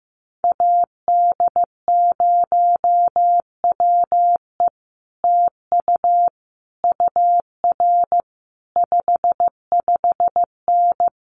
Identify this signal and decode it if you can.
Morse code, heard as a keyed tone, about 15 wpm, 700 Hz. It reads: AD0WE TU UR 55N
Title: Common QSO Phrases Encore 15wpm